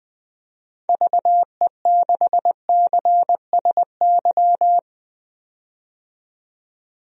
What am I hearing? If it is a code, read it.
Morse code, heard as a keyed tone, about 20 wpm, 700 Hz. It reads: VE6CSY
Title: DXCC-100 Call Signs - 6 Characters - 20wpm